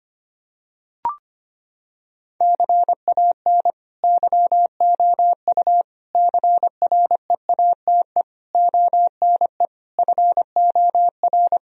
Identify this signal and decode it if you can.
Morse code, heard as a keyed tone, about 25 wpm, 700 Hz. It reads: CAN YOU CREATE ONE FOR
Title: Sentences from Top 1000 Words - Review Part 3 - 25wpm